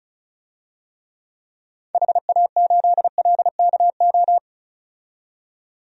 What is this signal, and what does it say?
Morse code, heard as a keyed tone, about 34 wpm, 700 Hz. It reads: HA8LKO
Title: DXCC-100 Call Signs - 6 Characters - 35wpm